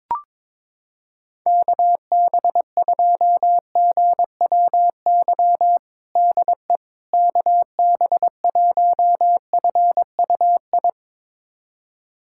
Morse code, heard as a keyed tone, about 22 wpm, 700 Hz. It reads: KB2GWY DE KB1FUI
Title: USA Callsign DE Callsign - 6 Characters - 22wpm